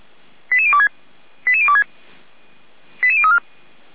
Five-tone selective call (ZVEI), heard as quick runs of five tones, tones of about 70 ms. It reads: 88026 88026 88034
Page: SDR/ZVEI